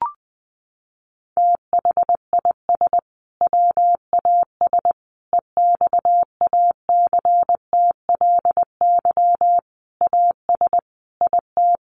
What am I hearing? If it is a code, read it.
Morse code, heard as a keyed tone, about 20 wpm, 700 Hz. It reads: THIS WAS EXACTLY AS IT
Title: Sentences from Top 900 Words 20wpm